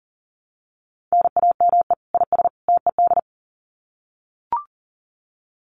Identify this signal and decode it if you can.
Morse code, heard as a keyed tone, about 40 wpm, 700 Hz. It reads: NAME IS TED
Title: Common QSO Phrases 40wpm